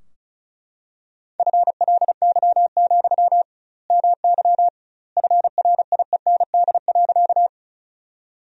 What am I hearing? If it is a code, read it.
Morse code, heard as a keyed tone, about 36 wpm, 700 Hz. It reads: FLY, MY FRIEND.
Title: 2x Word-Spacing - Sentences from Top 500 Words - 35wpm